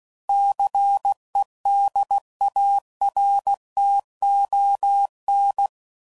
Morse code, heard as a keyed tone, about 16 wpm, 800 Hz. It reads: CEDARTON
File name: CW Code Groups @15WPM / fromSM's / Cities